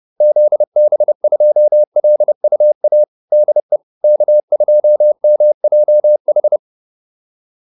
Morse code, heard as a keyed tone, about 30 wpm, 600 Hz. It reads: ZB2LUA DE K2MJH
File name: QSO402 30